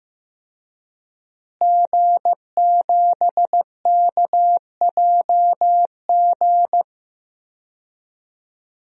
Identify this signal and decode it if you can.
Morse code, heard as a keyed tone, about 15 wpm, 700 Hz. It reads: G7KJG
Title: DXCC-100 Call Signs - 5 Characters - Repeated 3x - 15wpm